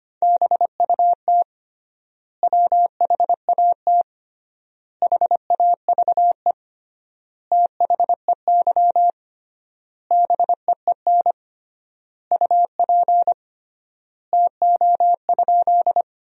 Morse code, heard as a keyed tone, about 25 wpm, 700 Hz. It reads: BUT WHAT HAVE THEY BEEN UP TO?
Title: 3x Word-Spacing - Sentences from Top 100 Words - 25wpm